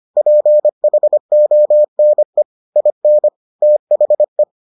Morse code, heard as a keyed tone, about 25 wpm, 600 Hz. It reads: PHONE IN THE